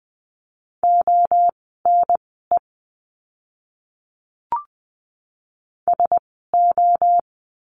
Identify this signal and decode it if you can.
Morse code, heard as a keyed tone, about 20 wpm, spaced as for 14 wpm, 700 Hz. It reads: ONE SO
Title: 054 Lesson 10f - Letter r - Words 20wpm